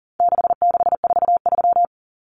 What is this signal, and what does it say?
Morse code, heard as a keyed tone, about 40 wpm, 700 Hz. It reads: 6643
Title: Common Cut Numbers - 4-Digit 40wpm